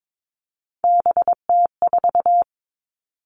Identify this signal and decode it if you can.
Morse code, heard as a keyed tone, about 22 wpm, 700 Hz. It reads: BT4
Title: Three Challenging Letter-Number Combinations 22wpm